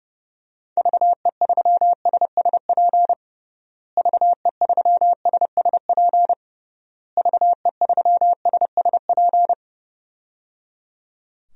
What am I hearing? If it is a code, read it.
Morse code, heard as a keyed tone, about 30 wpm, 700 Hz. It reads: VE3SSP VE3SSP VE3SSP
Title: DXCC-100 Call Signs - 6 Characters - Repeated 3x - 30wpm